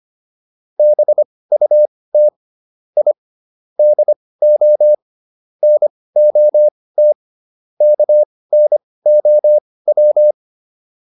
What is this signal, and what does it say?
Morse code, heard as a keyed tone, about 25 wpm, spaced as for 18 wpm, 600 Hz. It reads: BUT I DO NOT KNOW